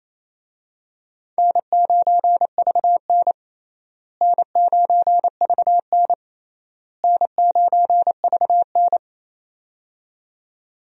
Morse code, heard as a keyed tone, about 28 wpm, 700 Hz. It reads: N9VN N9VN N9VN
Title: USA Call Signs - 4 Characters - Repeated 3x - 28wpm